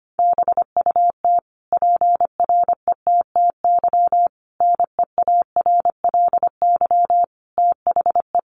Morse code, heard as a keyed tone, about 25 wpm, 700 Hz. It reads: BUT PRETTY NEARLY THE
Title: Sentences from Top 1300 Words 25wpm